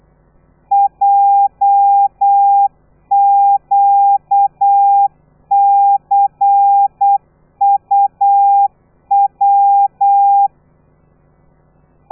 Morse code, heard as a keyed tone, about 8 wpm, 800 Hz. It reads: JQCUW
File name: telegrafia / i2tfj / Lez13 35